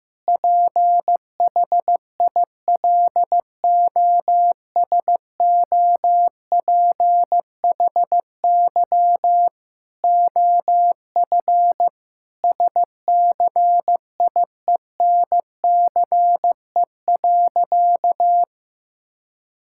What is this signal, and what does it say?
Morse code, heard as a keyed tone, about 15 wpm, 700 Hz. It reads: PHILOSOPHY OF SCIENCE.
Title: Sentences from Top 1000 Words - Review Part 6 - 15wpm